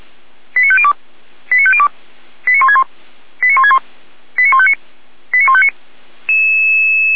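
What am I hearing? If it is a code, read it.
Five-tone selective call (ZVEI), heard as quick runs of five tones, tones of about 70 ms. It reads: 89502 89502 89161 89161 89159 89159
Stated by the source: ZVEI five-tone selective call received by SDR.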